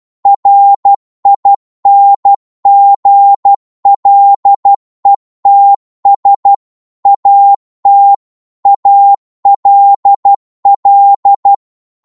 Morse code, heard as a keyed tone, about 12 wpm, 800 Hz. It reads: RINGLETS AT ALL
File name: 12wpm